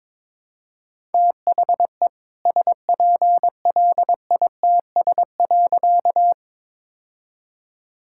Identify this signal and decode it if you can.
Morse code, heard as a keyed tone, about 22 wpm, 700 Hz. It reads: THE SPLITS.
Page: morse-code-ninja-practice-22wpm